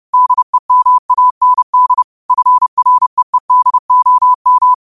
Morse code, heard as a keyed tone, about 30 wpm, 1000 Hz. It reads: DEMAND FREEDOM